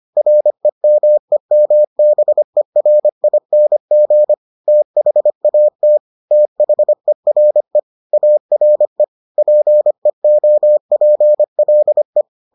Morse code, heard as a keyed tone, about 25 wpm, 600 Hz. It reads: REMEMBERING THAT THERE ARE PEOPLE